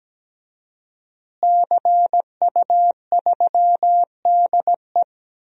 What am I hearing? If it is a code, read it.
Morse code, heard as a keyed tone, about 17 wpm, 700 Hz. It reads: CU3DE
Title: DXCC-100 Call Signs - 5 Characters - Repeated 3x - 17wpm